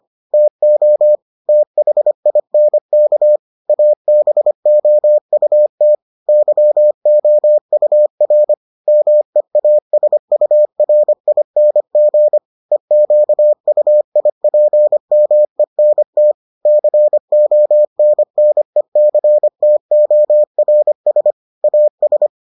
Morse code, heard as a keyed tone, about 25 wpm, 600 Hz. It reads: TO THINK ABOUT YOUR MEASURING EQUIPMENT CONNECTORS AS